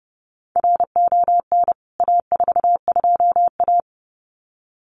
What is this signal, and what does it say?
Morse code, heard as a keyed tone, about 30 wpm, 700 Hz. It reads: RON A42A
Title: CWT Exchange 30wpm